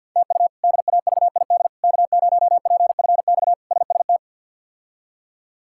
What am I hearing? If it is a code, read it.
Morse code, heard as a keyed tone, about 46 wpm, 700 Hz. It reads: TU DAVID K0PFX SST
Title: K1USN SST QSO Practice - 50wpm